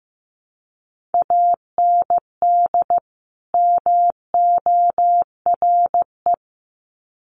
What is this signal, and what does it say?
Morse code, heard as a keyed tone, about 15 wpm, 700 Hz. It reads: AND MORE
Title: Sets of 2 Words 15wpm